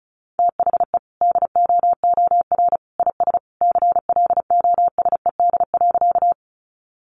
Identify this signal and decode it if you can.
Morse code, heard as a keyed tone, about 34 wpm, 700 Hz. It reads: THE DOOR IS CLOSED.
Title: Sentences from Top 1400 Words 35wpm